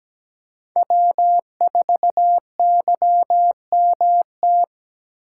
Morse code, heard as a keyed tone, about 17 wpm, 700 Hz. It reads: W4YMT
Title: USA Call Signs - 5 Characters - 2x Character Spacing - 17wpm